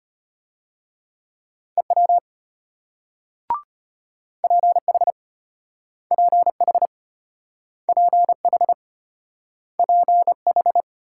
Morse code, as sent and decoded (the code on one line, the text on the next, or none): . .-- / .--. .... / .--. .... / .--. .... / .--. ....
EW PH PH PH PH